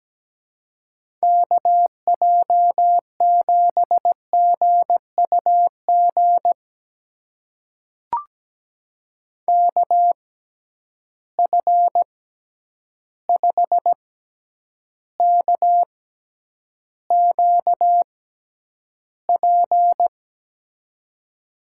Morse code, heard as a keyed tone, about 17 wpm, spaced as for 9 wpm, 700 Hz. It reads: KJ7GUG KF5KQP
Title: USA Call Signs - 6 Characters - 6x Character Spacing - 17wpm